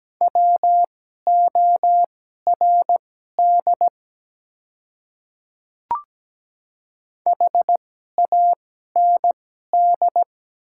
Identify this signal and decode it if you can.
Morse code, heard as a keyed tone, about 17 wpm, spaced as for 12 wpm, 700 Hz. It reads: WORD HAND
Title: 2x Character Spacing - Top 200 Words - 17wpm